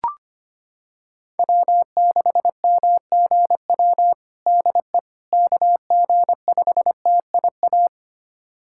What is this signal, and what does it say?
Morse code, heard as a keyed tone, about 25 wpm, 700 Hz. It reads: W6MGW DE KG5TIA
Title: USA Callsign DE Callsign 25wpm